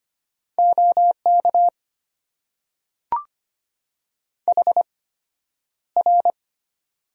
Morse code, heard as a keyed tone, about 25 wpm, spaced as for 11 wpm, 700 Hz. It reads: OK HR
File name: Top 90 QSO Elements - 8x Character Spacing - 25wpm